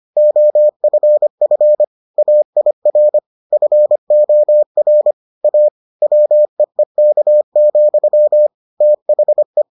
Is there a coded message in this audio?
Yes: Morse code from 0.2 to 9.6 s, heard as a keyed tone, about 25 wpm, 600 Hz. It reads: OFF AIR FOR A WEEK, THE